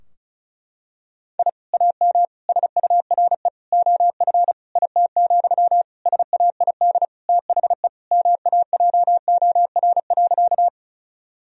Morse code, heard as a keyed tone, about 34 wpm, 700 Hz. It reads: I AM SURE OF IT, SAID THE MAJOR.
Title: Sentences from Top 1000 Words - Review Part 4 - 35wpm